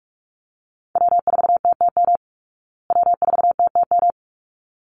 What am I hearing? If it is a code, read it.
Morse code, heard as a keyed tone, about 44 wpm, 700 Hz. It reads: W4TTM W4TTM
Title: Call Signs Repeated 3x - 45wpm